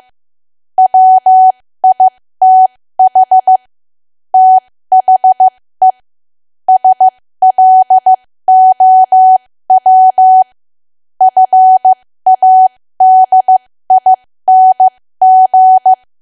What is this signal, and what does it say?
Morse code, heard as a keyed tone, about 15 wpm, spaced as for 13 wpm, 750 Hz. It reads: WITH THE SLOW FADING
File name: cwa / W1AW-Level2 / 13WPM